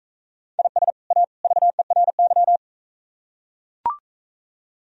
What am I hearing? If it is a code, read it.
Morse code, heard as a keyed tone, about 42 wpm, 700 Hz. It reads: IS A VERY